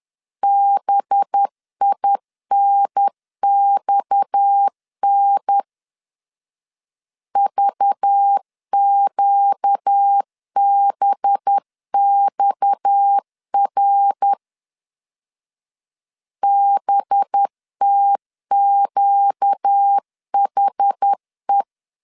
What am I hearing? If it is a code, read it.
Morse code, heard as a keyed tone, about 10 wpm, 800 Hz. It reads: BINXN VQBXR BTQHE